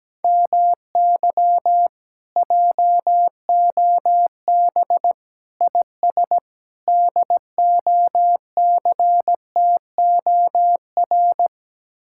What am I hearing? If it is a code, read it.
Morse code, heard as a keyed tone, about 17 wpm, 700 Hz. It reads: MY JOB IS DOCTOR
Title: Speed-Racing Common QSO Phrases Encore - 17wpm